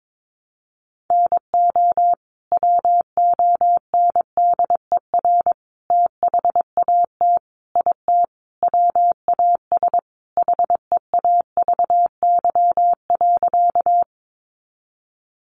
Morse code, heard as a keyed tone, about 22 wpm, 700 Hz. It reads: NO WONDER THAT IT WAS HEAVY.
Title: Sentences from Top 1100 Words 22wpm